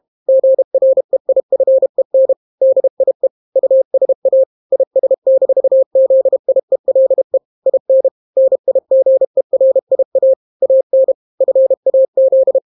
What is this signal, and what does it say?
Morse code, heard as a keyed tone, about 32 wpm, 530 Hz. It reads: GREIFEN DIE USA IS-ZIELE IN NIGERIA AN FAZ